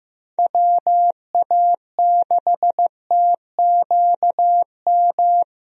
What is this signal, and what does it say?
Morse code, heard as a keyed tone, about 15 wpm, 700 Hz. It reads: WA6TQM
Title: USA Call Signs - 6 Characters - Repeated 2x - 15wpm